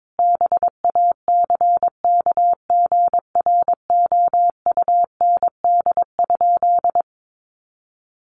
Morse code, heard as a keyed tone, about 22 wpm, 700 Hz. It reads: BACKGROUND?
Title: Sentences from Top 1300 Words 22wpm